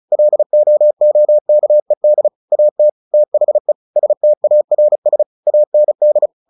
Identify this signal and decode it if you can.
Morse code, heard as a keyed tone, about 34 wpm, 600 Hz. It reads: LOOKED AT THE STARS AND
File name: the old man and the sea / the old man and the sea 1 35wpm